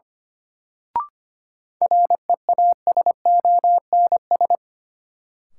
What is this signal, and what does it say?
Morse code, heard as a keyed tone, about 25 wpm, 700 Hz. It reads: REASONS
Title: Sets of 4 Words - New Word Drill - 25wpm